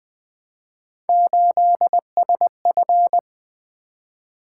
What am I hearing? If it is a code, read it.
Morse code, heard as a keyed tone, about 20 wpm, 700 Hz. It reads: 8SF
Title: Three Letter-Number Combinations - 20wpm